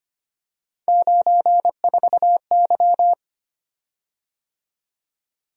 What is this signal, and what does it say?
Morse code, heard as a keyed tone, about 25 wpm, 700 Hz. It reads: 94Y
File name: Three Letter-Number Combinations - 25wpm